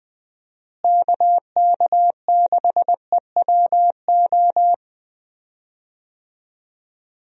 Morse code, heard as a keyed tone, about 20 wpm, 700 Hz. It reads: KK6EWO
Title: USA Call Signs - 6 Characters - Repeated 2x - 20wpm